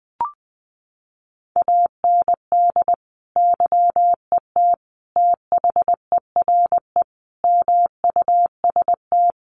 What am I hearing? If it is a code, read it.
Morse code, heard as a keyed tone, about 19 wpm, 700 Hz. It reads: AND YET THERE MUST
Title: Sentences from Top 1300 Words 20wpm